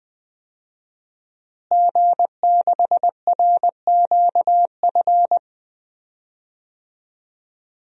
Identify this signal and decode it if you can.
Morse code, heard as a keyed tone, about 21 wpm, 700 Hz. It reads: G6RQF
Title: DXCC-100 Call Signs - 5 Characters - Repeated 3x - 20wpm